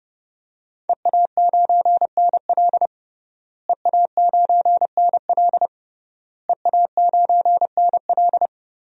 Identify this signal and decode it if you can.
Morse code, heard as a keyed tone, about 30 wpm, 700 Hz. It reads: EA9NL EA9NL EA9NL
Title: DXCC-100 Call Signs - 5 Characters - Repeated 3x - 30wpm